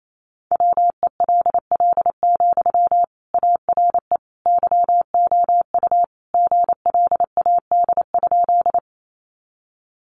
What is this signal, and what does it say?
Morse code, heard as a keyed tone, about 28 wpm, 700 Hz. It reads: WELL, ARE YOU GLAD?
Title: Sentences from Top 800 Words 28wpm